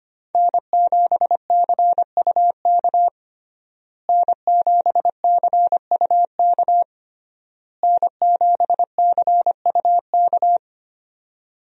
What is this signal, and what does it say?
Morse code, heard as a keyed tone, about 25 wpm, 700 Hz. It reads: N7CUK N7CUK N7CUK
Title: USA Call Signs - 5 Characters - Repeated 3x - 25wpm